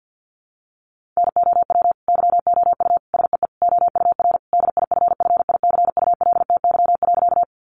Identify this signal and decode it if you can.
Morse code, heard as a keyed tone, about 50 wpm, 700 Hz. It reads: NOW YOU SEE OUR DIFFICULTY.
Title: Sentences from Top 1500 Words 50wpm